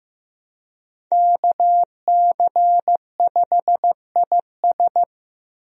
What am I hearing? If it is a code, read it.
Morse code, heard as a keyed tone, about 16 wpm, 700 Hz. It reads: KC5IS
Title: USA Call Signs - 5 Characters - Repeated 3x - 15wpm